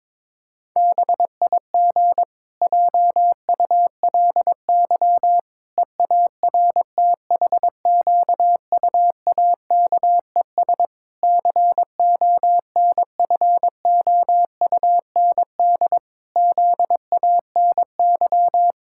Morse code, heard as a keyed tone, about 22 wpm, 700 Hz. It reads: BIG JULY EARTHQUAKES CONFOUND ZANY
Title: Pangram Challenge v2 - 22wpm